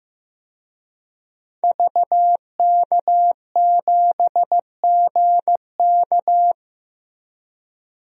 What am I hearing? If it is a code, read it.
Morse code, heard as a keyed tone, about 15 wpm, 700 Hz. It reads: VK7GK
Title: DXCC-100 Call Signs - 5 Characters - 15wpm